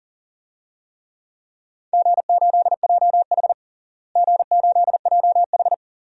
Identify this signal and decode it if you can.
Morse code, heard as a keyed tone, about 40 wpm, 700 Hz. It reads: G8JH G8JH
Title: DXCC-100 Call Signs - 4 Characters - Repeated 3x - 40wpm